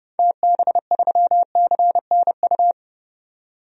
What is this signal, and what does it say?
Morse code, heard as a keyed tone, about 30 wpm, 700 Hz. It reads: TB3CNU
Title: DXCC-100 Call Signs - 6 Characters - 30wpm